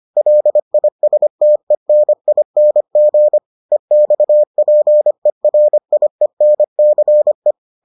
Morse code, heard as a keyed tone, about 25 wpm, 600 Hz. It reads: LISTENING EXPERIENCE